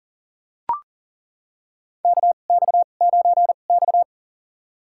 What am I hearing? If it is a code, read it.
Morse code, heard as a keyed tone, about 37 wpm, spaced as for 28 wpm, 700 Hz. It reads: KX9X
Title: USA Call Signs - 4 Characters - 2x Character Spacing - 40wpm